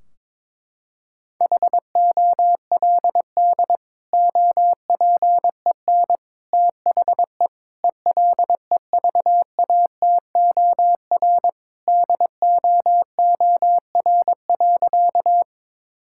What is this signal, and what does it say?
Morse code, heard as a keyed tone, about 22 wpm, 700 Hz. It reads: HOLD OPEN THE ELEVATOR DOOR.
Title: Sentences from Top 1600 Words 22wpm